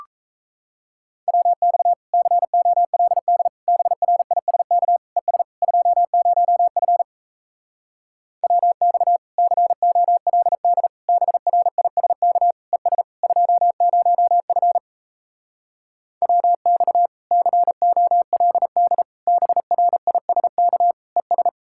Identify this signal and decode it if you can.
Morse code, heard as a keyed tone, about 37 wpm, 700 Hz. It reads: WX COLD BRISK ES 20F WX COLD BRISK ES 20F WX COLD BRISK ES